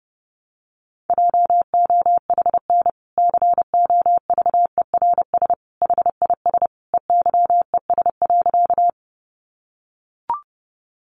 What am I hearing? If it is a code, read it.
Morse code, heard as a keyed tone, about 28 wpm, 700 Hz. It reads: JOHN COVERS HIS EYES.
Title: Easy Phrases 30wpm